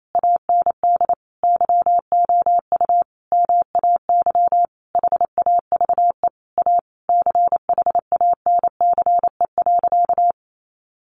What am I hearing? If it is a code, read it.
Morse code, heard as a keyed tone, about 28 wpm, 700 Hz. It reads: AND YOU MAY HAVE A CHANCE.
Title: Sentences from Top 1400 Words 28wpm